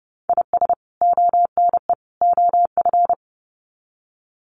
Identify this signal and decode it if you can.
Morse code, heard as a keyed tone, about 30 wpm, 700 Hz. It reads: IS ONE OF
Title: Sets of 3 Words 30wpm